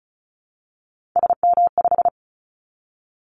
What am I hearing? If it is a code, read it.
Morse code, heard as a keyed tone, about 34 wpm, 700 Hz. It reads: SM5
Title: Callsign Prefixes for Top 25 DXCC - 35wpm